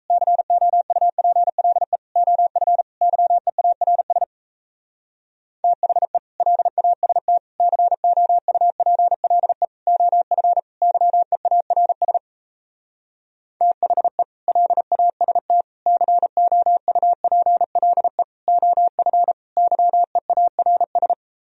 Morse code, heard as a keyed tone, about 38 wpm, 700 Hz. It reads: COUPLE OF YEARS THE LAST COUPLE OF YEARS THE LAST COUPLE OF YEARS